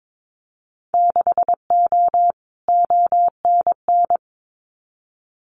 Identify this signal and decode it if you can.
Morse code, heard as a keyed tone, about 22 wpm, 700 Hz. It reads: 6O ONN
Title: Winter Field Day Exchanges Encore 22wpm